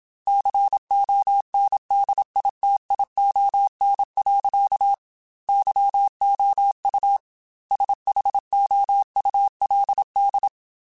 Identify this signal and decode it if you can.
Morse code, heard as a keyed tone, about 27 wpm, 800 Hz. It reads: CONDITION. YOU SHOULD
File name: Technical Subjects / Technical Subjects-mp3s / OR Gate @25